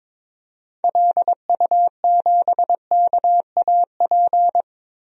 Morse code, heard as a keyed tone, about 22 wpm, 700 Hz. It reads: LU7KAP
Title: DXCC-100 Call Signs - 6 Characters - 22wpm